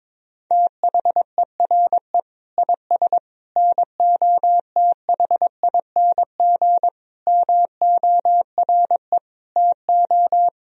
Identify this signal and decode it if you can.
Morse code, heard as a keyed tone, about 22 wpm, 700 Hz. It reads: THERE IS NOTHING MORE TO
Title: Sentences from Top 500 Words and Abbreviations 22wpm